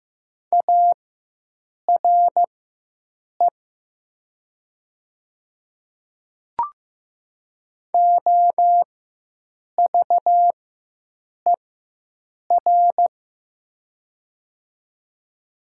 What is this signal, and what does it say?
Morse code, heard as a keyed tone, about 15 wpm, spaced as for 7 wpm, 700 Hz. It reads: ARE OVER
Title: 4x Character Spacing - Top 100 Words - 15wpm